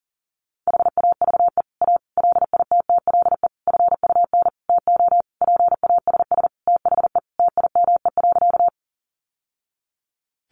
Morse code, heard as a keyed tone, about 40 wpm, 700 Hz. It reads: HAVE A LITTLE FUN TO PASS THE TIME.
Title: Sentences from Top 700 Words 40wpm